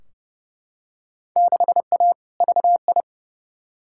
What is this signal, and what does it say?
Morse code, heard as a keyed tone, about 31 wpm, 700 Hz. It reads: 6A VI
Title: Field Day Exchanges Encore - 30wpm